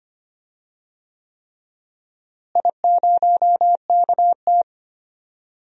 Morse code, heard as a keyed tone, about 25 wpm, 700 Hz. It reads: I0KT
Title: DXCC-100 Call Signs - 4 Characters - Repeated 3x - 25wpm